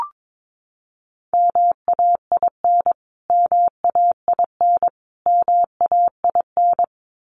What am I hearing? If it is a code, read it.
Morse code, heard as a keyed tone, about 22 wpm, 700 Hz. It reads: MAIN MAIN MAIN
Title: Top 500 Words Repeated 22wpm